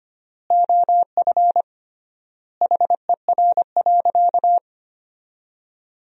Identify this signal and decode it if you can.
Morse code, heard as a keyed tone, about 25 wpm, 700 Hz. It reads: OF HER.
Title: 3x Word-Spacing - Sentences from Top 200 Words - 25wpm